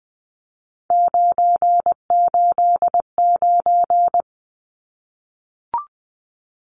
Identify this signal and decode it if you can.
Morse code, heard as a keyed tone, about 20 wpm, 700 Hz. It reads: 989
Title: Numbers - 3-Digit 20wpm